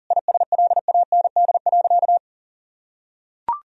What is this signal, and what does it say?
Morse code, heard as a keyed tone, about 40 wpm, 700 Hz. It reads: ISLAND.
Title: Sentences from Top 1000 Words - Review Part 3 - 40wpm